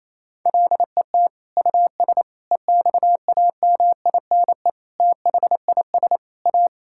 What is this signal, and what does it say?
Morse code, heard as a keyed tone, about 28 wpm, 700 Hz. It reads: LET US EXAMINE THIS A
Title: Sentences from Top 1600 Words 28wpm